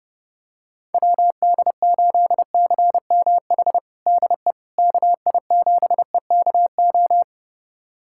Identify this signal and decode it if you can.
Morse code, heard as a keyed tone, about 30 wpm, 700 Hz. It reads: WD8CMH DE KI7EKO
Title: USA Callsign DE Callsign 30wpm